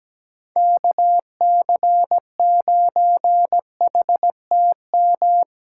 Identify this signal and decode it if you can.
Morse code, heard as a keyed tone, about 17 wpm, 700 Hz. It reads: KC9HTM
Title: USA Callsign DE Callsign 17wpm